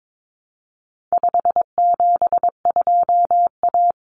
Morse code, heard as a keyed tone, about 22 wpm, 700 Hz. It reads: 572A
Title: Common Cut Numbers - 4-Digit 22wpm